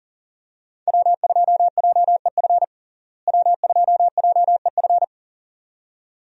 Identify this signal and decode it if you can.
Morse code, heard as a keyed tone, about 37 wpm, 700 Hz. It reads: W2JEF W2JEF
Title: USA Call Signs - 5 Characters - Repeated 2x - 40wpm